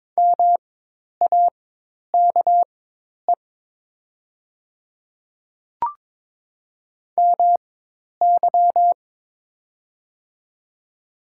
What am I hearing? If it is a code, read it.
Morse code, heard as a keyed tone, about 22 wpm, spaced as for 10 wpm, 700 Hz. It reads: MAKE MY
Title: 4x Character Spacing - Top 200 Words - 22wpm